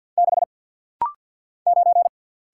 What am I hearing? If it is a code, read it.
Morse code, heard as a keyed tone, about 50 wpm, 700 Hz. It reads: 6 9
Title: Single Letter-Number - Rapid-Fire 50wpm